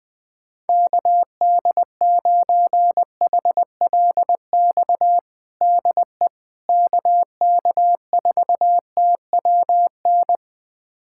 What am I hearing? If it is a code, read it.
Morse code, heard as a keyed tone, about 20 wpm, 700 Hz. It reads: KD9HLX DE KK4TWN
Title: USA Callsign DE Callsign - 6 Characters - 20wpm